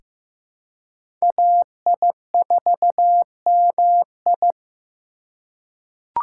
Morse code, heard as a keyed tone, about 15 wpm, 700 Hz. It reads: AI4MI
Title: K1USN SST QSO Practice - 15wpm